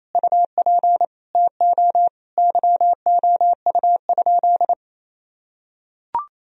Morse code, heard as a keyed tone, about 28 wpm, 700 Hz. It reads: UP TO YOU?
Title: Sentences from Top 1000 Words - Review Part 2 - 28wpm